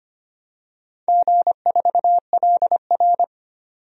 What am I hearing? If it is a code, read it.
Morse code, heard as a keyed tone, about 25 wpm, 700 Hz. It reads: G4LR
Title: DXCC-100 Call Signs - 4 Characters - 25wpm